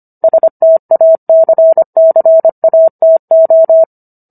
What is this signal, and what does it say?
Morse code, heard as a keyed tone, about 25 wpm, 650 Hz. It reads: STACCATO